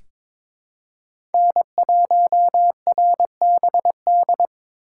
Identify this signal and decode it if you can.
Morse code, heard as a keyed tone, about 22 wpm, 700 Hz. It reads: N1RBD
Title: K1USN SST QSO Practice - 22wpm